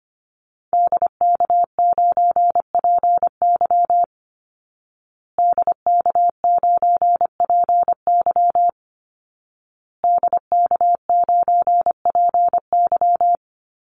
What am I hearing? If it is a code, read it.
Morse code, heard as a keyed tone, about 25 wpm, 700 Hz. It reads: DK9PY DK9PY DK9PY
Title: Call Signs Repeated 2x - 25wpm